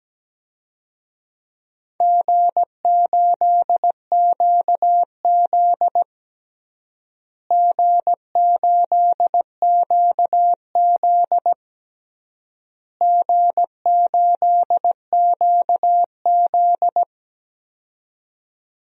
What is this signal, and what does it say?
Morse code, heard as a keyed tone, about 17 wpm, 700 Hz. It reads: G8QZ G8QZ G8QZ
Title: DXCC-100 Call Signs - 4 Characters - Repeated 3x - 17wpm